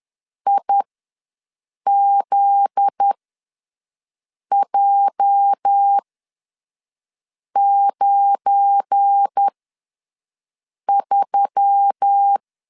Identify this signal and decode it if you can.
Morse code, heard as a keyed tone, about 10 wpm, spaced as for 5 wpm, 800 Hz. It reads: IZJ93
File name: LEZIONI / 8 YJ45 / 8 YJ ws 10wpm 4043